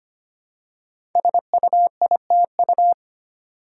Morse code, heard as a keyed tone, about 25 wpm, 700 Hz. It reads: SUITU
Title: Code Groups - 5 Letters - 25wpm